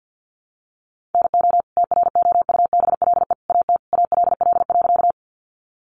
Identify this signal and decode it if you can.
Morse code, heard as a keyed tone, about 50 wpm, 700 Hz. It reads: NO TROUBLE AT ALL.
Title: Sentences from Top 1200 Words 50wpm